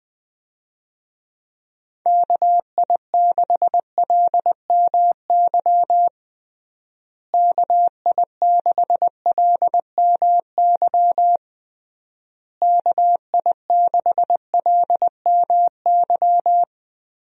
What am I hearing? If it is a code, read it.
Morse code, heard as a keyed tone, about 21 wpm, 700 Hz. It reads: KI6LMY KI6LMY KI6LMY
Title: USA Call Signs - 6 Characters - Repeated 3x - 20wpm